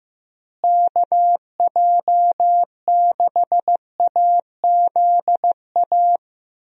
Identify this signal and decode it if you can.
Morse code, heard as a keyed tone, about 15 wpm, 700 Hz. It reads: KJ6AZA
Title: USA Call Signs - 6 Characters - Repeated 3x - 15wpm